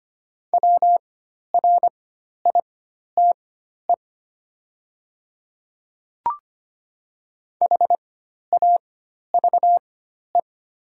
Morse code, heard as a keyed tone, about 25 wpm, spaced as for 12 wpm, 700 Hz. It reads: WRITE HAVE
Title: 4x Character Spacing - Top 100 Words - 25wpm